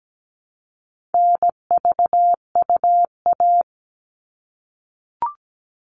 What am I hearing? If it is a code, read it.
Morse code, heard as a keyed tone, about 17 wpm, 700 Hz. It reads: NVUA
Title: All Cut Numbers - 4-Digit 17wpm